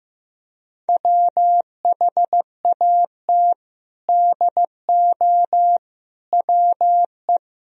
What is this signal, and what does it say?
Morse code, heard as a keyed tone, about 15 wpm, 700 Hz. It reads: WHAT DO WE
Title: Sentences from Top 1000 Words - Review Part 2 - 15wpm